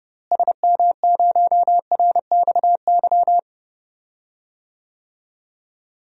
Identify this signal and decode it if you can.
Morse code, heard as a keyed tone, about 30 wpm, 700 Hz. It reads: SM0RXY
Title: DXCC-100 Call Signs - 6 Characters - Repeated 3x - 30wpm